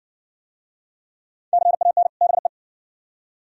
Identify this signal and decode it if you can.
Morse code, heard as a keyed tone, about 52 wpm, 700 Hz. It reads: CAN BE